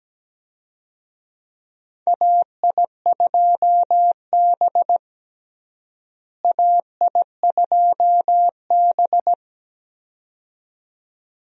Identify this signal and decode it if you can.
Morse code, heard as a keyed tone, about 17 wpm, 700 Hz. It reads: AI2B AI2B
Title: USA Call Signs - 4 Characters - Repeated 2x - 17wpm